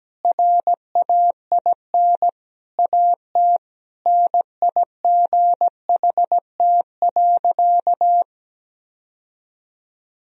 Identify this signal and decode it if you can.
Morse code, heard as a keyed tone, about 17 wpm, 700 Hz. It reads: RAIN AT NIGHT.
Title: Sentences from Top 1000 Words - Review Part 1 - 17wpm